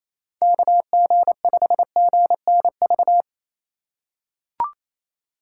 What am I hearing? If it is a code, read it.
Morse code, heard as a keyed tone, about 28 wpm, 700 Hz. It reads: KG5GNV
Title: USA Call Signs - 6 Characters - 2x Character Spacing - 28wpm